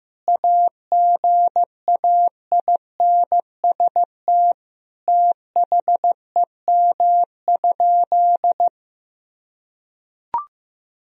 Morse code, heard as a keyed tone, about 15 wpm, 700 Hz. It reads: AGAINST THEM?
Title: Sentences from Top 1000 Words - Review Part 2 - 15wpm